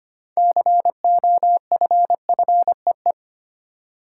Morse code, heard as a keyed tone, about 25 wpm, 700 Hz. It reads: COFFEE
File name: practice / 25wpm / Easy Words with Hints - 25wpm